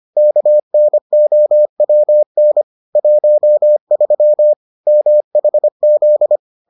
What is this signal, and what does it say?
Morse code, heard as a keyed tone, about 25 wpm, 600 Hz. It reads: KNOWN 13 MHZ